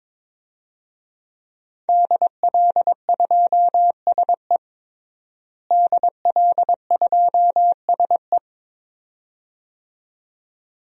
Morse code, heard as a keyed tone, about 21 wpm, 700 Hz. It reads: DL2SE DL2SE
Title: DXCC-100 Call Signs - 5 Characters - Repeated 2x - 22wpm